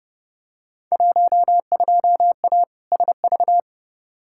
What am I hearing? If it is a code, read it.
Morse code, heard as a keyed tone, about 30 wpm, 700 Hz. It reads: 12A SV
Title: Field Day Exchanges Encore - 30wpm